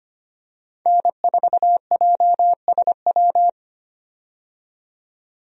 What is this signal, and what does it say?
Morse code, heard as a keyed tone, about 25 wpm, 700 Hz. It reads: N4JSW
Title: USA Call Signs - 5 Characters - 25wpm